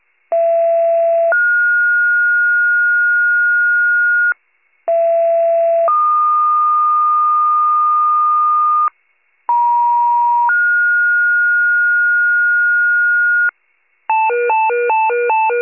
Two-tone page, heard as a one-second tone, then a three-second tone, about 670 then 1430 Hz, about 670 then 1220 Hz, about 950 then 1430 Hz, with noise. It is heard from 0.3 to 13.5 s.